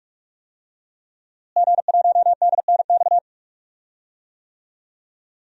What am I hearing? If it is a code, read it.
Morse code, heard as a keyed tone, about 44 wpm, 700 Hz. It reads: G1DNX
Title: DXCC-100 Call Signs - 5 Characters - Repeated 3x - 45wpm